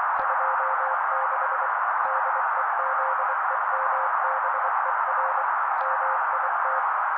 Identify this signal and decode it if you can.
Morse code, heard as a keyed tone, about 24 wpm, 530 Hz. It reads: 26 DEZEMBER MIT
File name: daily / 23wpm noise